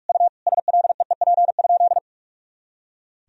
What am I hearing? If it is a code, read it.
Morse code, heard as a keyed tone, about 46 wpm, 700 Hz. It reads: U SLEEP?
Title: Sentences from Top 500 Words and Abbreviations 45wpm